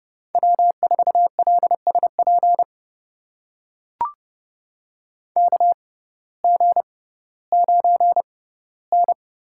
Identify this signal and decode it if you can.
Morse code, heard as a keyed tone, about 30 wpm, spaced as for 17 wpm, 700 Hz. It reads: W4LSP KG9N
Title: USA Call Signs - 6x Character Spacing - 30wpm